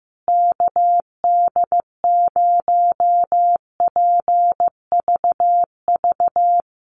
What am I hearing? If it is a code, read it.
Morse code, heard as a keyed tone, about 14 wpm, 700 Hz. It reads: KD0PVV
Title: USA Call Signs Repeated 3x 15wpm